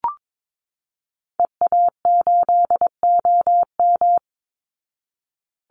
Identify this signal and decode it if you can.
Morse code, heard as a keyed tone, about 22 wpm, 700 Hz. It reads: EA8OM
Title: Call Signs Repeated 3x - 22wpm